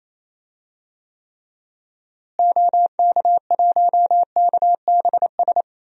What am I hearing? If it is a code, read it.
Morse code, heard as a keyed tone, about 28 wpm, 700 Hz. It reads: OK1KBS
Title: DXCC-100 Call Signs - 6 Characters - Repeated 2x - 28wpm